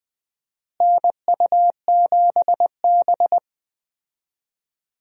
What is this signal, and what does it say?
Morse code, heard as a keyed tone, about 20 wpm, 700 Hz. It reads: NU7B
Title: USA Call Signs - 4 Characters - Repeated 3x - 20wpm